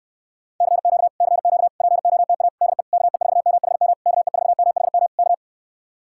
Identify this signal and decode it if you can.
Morse code, heard as a keyed tone, about 69 wpm, 700 Hz. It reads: CQ CQ POTA DE KE4MHM KE4MHM K